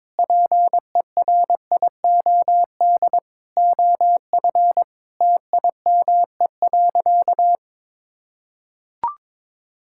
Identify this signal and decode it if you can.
Morse code, heard as a keyed tone, about 22 wpm, 700 Hz. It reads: PERIOD OF TIME.
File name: Sentences with 7-Word Phrases - 22wpm